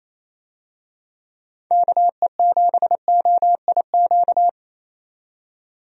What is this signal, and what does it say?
Morse code, heard as a keyed tone, about 28 wpm, 700 Hz. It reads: KE7OIQ
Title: USA Call Signs - 6 Characters - 28wpm